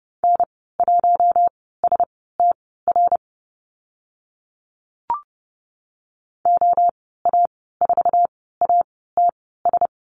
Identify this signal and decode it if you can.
Morse code, heard as a keyed tone, about 30 wpm, spaced as for 17 wpm, 700 Hz. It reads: N1STR OA4ATS
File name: Lesson 14g - Number 2 - Call Signs 30wpm